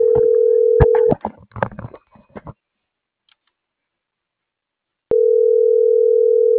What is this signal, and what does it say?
Ringback tone (North America), 440 + 480 Hz.